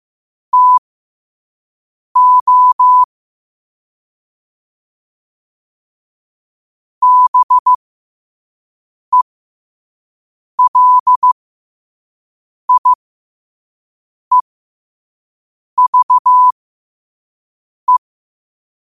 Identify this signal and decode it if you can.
Morse code, heard as a keyed tone, about 15 wpm, spaced as for 5 wpm, 1000 Hz. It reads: TO BELIEVE
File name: qotdfiles / 5wpm